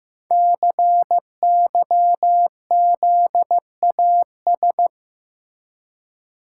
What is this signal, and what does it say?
Morse code, heard as a keyed tone, about 15 wpm, 700 Hz. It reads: CYZAS
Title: Code Groups - 5 Letters - 15wpm